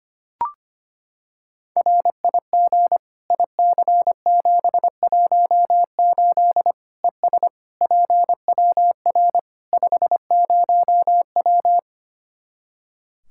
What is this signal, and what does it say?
Morse code, heard as a keyed tone, about 25 wpm, 700 Hz. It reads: RIG IC718 ES PWR 50W
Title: Practice QSO - Exchange 2 - Info - 25wpm